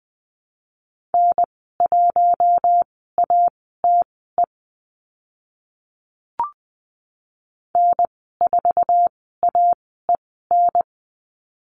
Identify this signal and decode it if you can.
Morse code, heard as a keyed tone, about 20 wpm, spaced as for 15 wpm, 700 Hz. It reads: N1ATE N4AEN
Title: 062 Lesson 11g - Letter h - Call Signs 20wpm